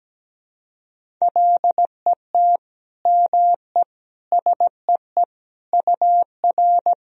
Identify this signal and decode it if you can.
Morse code, heard as a keyed tone, about 17 wpm, 700 Hz. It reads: LET ME SEE UR
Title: Sentences from Top 500 Words and Abbreviations 17wpm